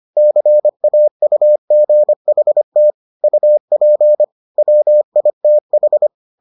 Morse code, heard as a keyed tone, about 25 wpm, 600 Hz. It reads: CAUGHT UP WITH